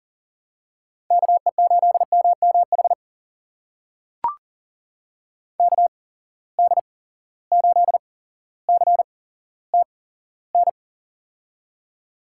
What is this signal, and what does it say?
Morse code, heard as a keyed tone, about 40 wpm, spaced as for 17 wpm, 700 Hz. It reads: KE8MMH KD8CTN
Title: USA Call Signs - 6 Characters - 8x Character Spacing - 40wpm